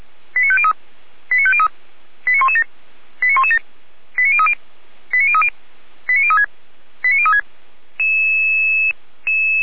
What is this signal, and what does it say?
Five-tone selective call (ZVEI), heard as quick runs of five tones, tones of about 70 ms. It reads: 89503 89503 89117 89117 89030 89030 89036 89036